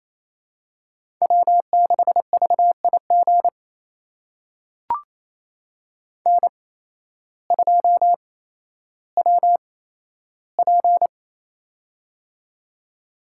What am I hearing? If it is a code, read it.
Morse code, heard as a keyed tone, about 28 wpm, spaced as for 13 wpm, 700 Hz. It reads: W6VIG N2WP
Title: USA Call Signs - 8x Character Spacing - 28wpm